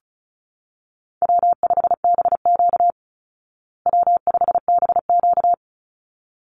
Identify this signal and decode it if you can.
Morse code, heard as a keyed tone, about 34 wpm, 700 Hz. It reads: W5BQ W5BQ
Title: Call Signs Repeated 3x - 35wpm